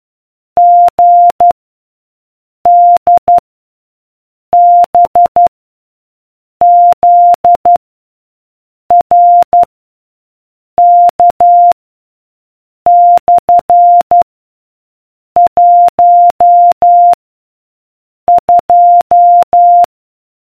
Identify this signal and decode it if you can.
Morse code, heard as a keyed tone, about 12 wpm, 700 Hz. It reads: G D B Z R K / 1 2